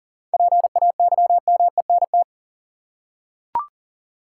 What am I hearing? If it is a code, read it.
Morse code, heard as a keyed tone, about 40 wpm, 700 Hz. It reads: PAYMENT
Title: Seven Letter Words - 40wpm